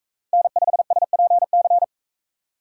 Morse code, heard as a keyed tone, about 41 wpm, 700 Hz. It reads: N5SPC